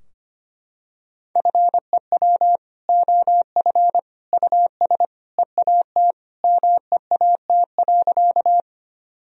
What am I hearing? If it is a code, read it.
Morse code, heard as a keyed tone, about 25 wpm, 700 Hz. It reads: FEW OF US EAT MEAT.
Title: Sentences from Top 1000 Words - Review Part 3 - 25wpm